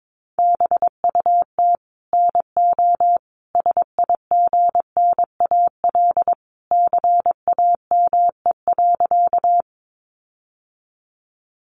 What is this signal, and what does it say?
Morse code, heard as a keyed tone, about 22 wpm, 700 Hz. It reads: BUT NO SIGNAL CAME.
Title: Sentences from Top 1400 Words 22wpm